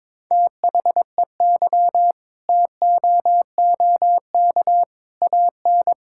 Morse code, heard as a keyed tone, about 22 wpm, 700 Hz. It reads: THEY TOOK AN
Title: Sentences from Top 1000 Words - Review Part 6 - 22wpm